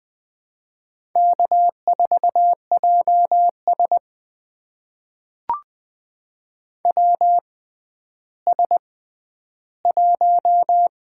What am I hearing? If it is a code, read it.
Morse code, heard as a keyed tone, about 20 wpm, spaced as for 11 wpm, 700 Hz. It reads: K4JS WS1
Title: USA Call Signs - 4 Characters - 6x Character Spacing - 20wpm